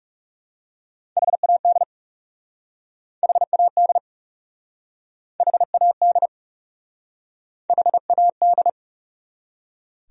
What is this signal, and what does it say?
Morse code, heard as a keyed tone, about 40 wpm, 700 Hz. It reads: HAD HAD HAD HAD